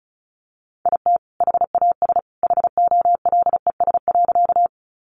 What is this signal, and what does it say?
Morse code, heard as a keyed tone, about 34 wpm, 700 Hz. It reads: IT HAS HOLES.
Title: Additional Easy Phrases 35wpm